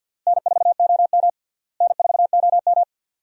Morse code, heard as a keyed tone, about 51 wpm, 700 Hz. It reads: N4OM N4OM
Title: USA Call Signs - 4 Characters - Repeated 2x - 50wpm